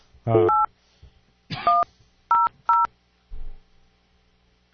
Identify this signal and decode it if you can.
DTMF, heard as telephone touch tones, tones of about 190 ms. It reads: 8100